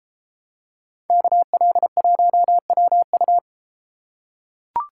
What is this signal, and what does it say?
Morse code, heard as a keyed tone, about 31 wpm, 700 Hz. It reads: KL1WU